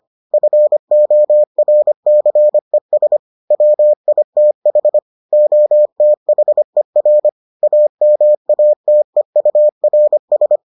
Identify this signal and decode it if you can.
Morse code, heard as a keyed tone, about 25 wpm, 600 Hz. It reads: FORCES WITH OTHER AMATEURS